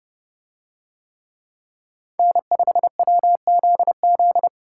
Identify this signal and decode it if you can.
Morse code, heard as a keyed tone, about 30 wpm, 700 Hz. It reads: N5WZZ